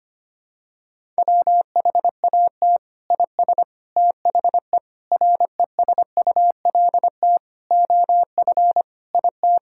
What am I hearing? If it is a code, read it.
Morse code, heard as a keyed tone, about 25 wpm, 700 Hz. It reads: WHAT IS THE RESULT OF IT
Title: Sentences from Top 1000 Words - Review Part 5 - 25wpm